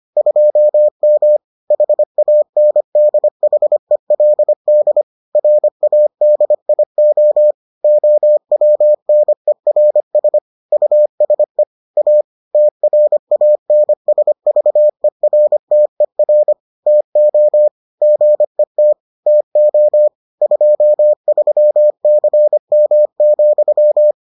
Morse code, heard as a keyed tone, about 25 wpm, 600 Hz. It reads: 2M HANDHELD RADIO OWNERS USE A TRANSVERTER TO GET TO 23CM,